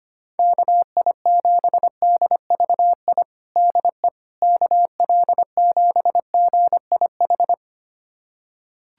Morse code, heard as a keyed tone, about 25 wpm, 700 Hz. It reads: KI7DVI DE KL7GIH
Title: USA Callsign DE Callsign - 6 Characters - 25wpm